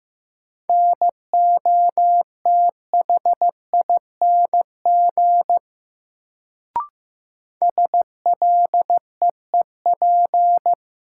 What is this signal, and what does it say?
Morse code, heard as a keyed tone, about 15 wpm, 700 Hz. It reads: NOTHING SLEEP
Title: Sentences from Top 500 Words - New Word Drill - 15wpm